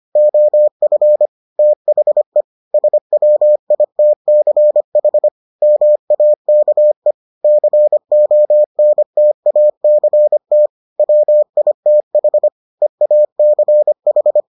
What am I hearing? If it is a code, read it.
Morse code, heard as a keyed tone, about 25 wpm, 600 Hz. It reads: OF THE SWITCH MAKE CONTACT WITH EACH